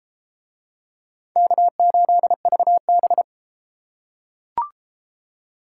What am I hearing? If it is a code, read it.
Morse code, heard as a keyed tone, about 31 wpm, 700 Hz. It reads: K8VB